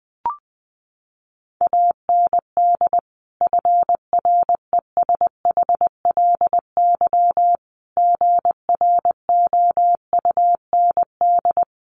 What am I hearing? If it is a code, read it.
Morse code, heard as a keyed tone, about 20 wpm, 700 Hz. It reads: AND FRESHLY GROUND
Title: Sets of 5 Words Encore 20wpm